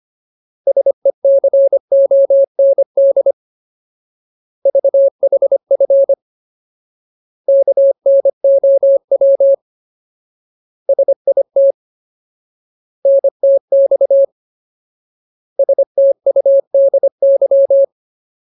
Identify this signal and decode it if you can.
Morse code, heard as a keyed tone, about 25 wpm, 550 Hz. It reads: SECOND VHF KNOW SIT NTX STUDY
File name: Ultimate Word List / 25WPM Ultimate